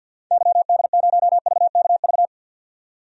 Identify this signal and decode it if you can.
Morse code, heard as a keyed tone, about 48 wpm, 700 Hz. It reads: YD0VKV
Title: DXCC-100 Call Signs - 6 Characters - 50wpm